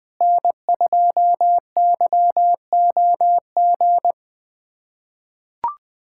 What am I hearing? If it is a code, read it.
Morse code, heard as a keyed tone, about 20 wpm, 700 Hz. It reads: N2YOG
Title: USA Callsign DE Callsign 20wpm